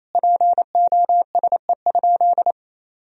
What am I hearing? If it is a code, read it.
Morse code, heard as a keyed tone, about 28 wpm, 700 Hz. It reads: POSE?
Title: Sentences from Top 1000 Words - Review Part 2 - 28wpm